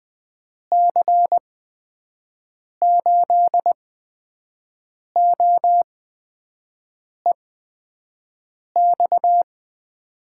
Morse code, heard as a keyed tone, about 20 wpm, spaced as for 5 wpm, 700 Hz. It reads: C8OEX
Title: USA Call Signs - 6 Characters - 8x Character Spacing - 20wpm